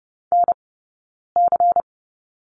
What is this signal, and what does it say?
Morse code, heard as a keyed tone, about 30 wpm, spaced as for 9 wpm, 690 Hz. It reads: NC
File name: Lesson 17d - Letter u - Two Characters 30wpm